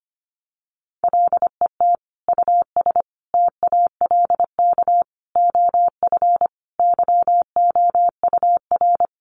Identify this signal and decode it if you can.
Morse code, heard as a keyed tone, about 25 wpm, 700 Hz. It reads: LET US TALK OF YOUR
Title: Sentences from Top 1000 Words 25wpm